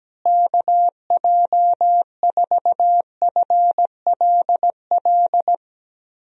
Morse code, heard as a keyed tone, about 17 wpm, 700 Hz. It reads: KJ4FLL
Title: USA Call Signs - 6 Characters - Repeated 3x - 17wpm